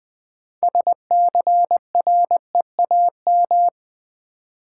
Morse code, heard as a keyed tone, about 20 wpm, 700 Hz. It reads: SCREAM
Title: Build A Word - 2 Letters At A Time - 6-Letter Words - 20wpm